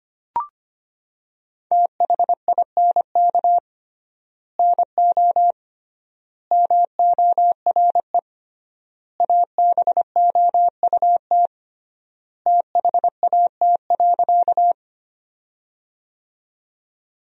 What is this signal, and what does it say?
Morse code, heard as a keyed tone, about 25 wpm, 700 Hz. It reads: THINK NO MORE ABOUT THAT.
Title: 3x Word-Spacing - Sentences from Top 200 Words - 25wpm